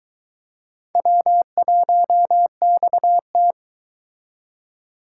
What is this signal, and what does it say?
Morse code, heard as a keyed tone, about 22 wpm, 700 Hz. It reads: W1XT